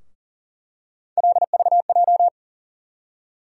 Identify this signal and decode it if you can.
Morse code, heard as a keyed tone, about 40 wpm, 700 Hz. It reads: LVJ
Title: USA Call Sign Postfixes - 3 Letters 40wpm